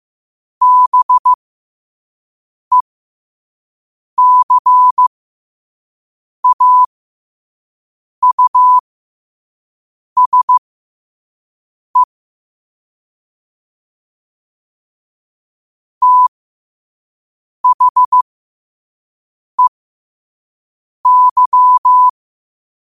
Morse code, heard as a keyed tone, about 15 wpm, spaced as for 5 wpm, 1000 Hz. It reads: BECAUSE THEY
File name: qotdfiles / 5wpm